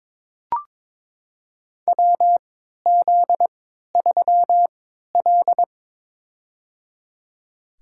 Morse code, heard as a keyed tone, about 22 wpm, spaced as for 13 wpm, 700 Hz. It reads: WZ3L
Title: USA Call Signs - 3x Character Spacing - 22wpm